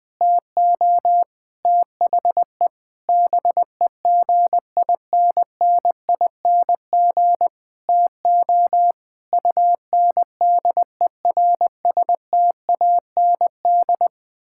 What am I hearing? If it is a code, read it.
Morse code, heard as a keyed tone, about 20 wpm, 700 Hz. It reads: TO THE BEGINNING TO UNDERSTAND
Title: Sentences with 8-Word Phrases - 20wpm